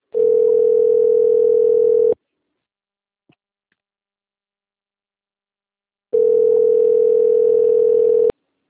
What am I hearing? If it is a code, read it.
Ringback tone (North America), 440 + 480 Hz.